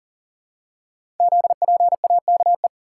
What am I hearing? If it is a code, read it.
Morse code, heard as a keyed tone, about 38 wpm, 690 Hz. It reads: ZPAKE
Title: Code Groups - 5 Letter:Numbers - 40wpm